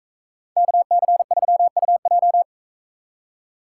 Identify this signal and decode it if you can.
Morse code, heard as a keyed tone, about 42 wpm, 700 Hz. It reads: KC3UJ